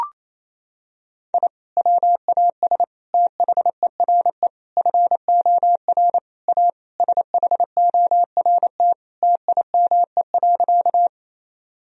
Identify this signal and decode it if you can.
Morse code, heard as a keyed tone, about 28 wpm, 700 Hz. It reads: I WAS THERE FOR A SHORT TIME.
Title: Sentences from Top 1000 Words - Review Part 2 - 28wpm